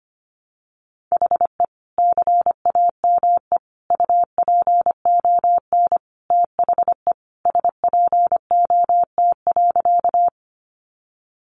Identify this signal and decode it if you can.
Morse code, heard as a keyed tone, about 25 wpm, 700 Hz. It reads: HE CAME UPON THE SPOT.
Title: Sentences from Top 1100 Words 25wpm